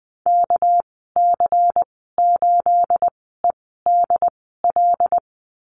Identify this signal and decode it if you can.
Morse code, heard as a keyed tone, about 20 wpm, spaced as for 15 wpm, 700 Hz. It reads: KC8EDL
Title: 238 Lesson 38g - Letter q - Call Signs 20wpm